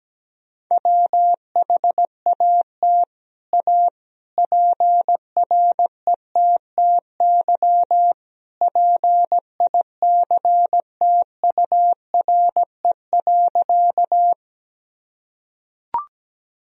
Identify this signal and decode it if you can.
Morse code, heard as a keyed tone, about 17 wpm, 700 Hz. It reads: WHAT A PRETTY PICTURE.
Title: Sentences from Top 1000 Words - Review Part 1 - 17wpm